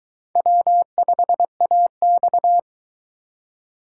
Morse code, heard as a keyed tone, about 22 wpm, 700 Hz. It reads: W5AX